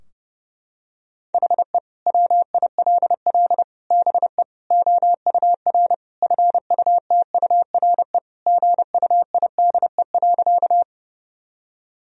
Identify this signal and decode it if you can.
Morse code, heard as a keyed tone, about 30 wpm, 700 Hz. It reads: HE WILL BE OUR FUTURE GUIDE.
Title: Sentences from Top 1000 Words - Review Part 3 - 30wpm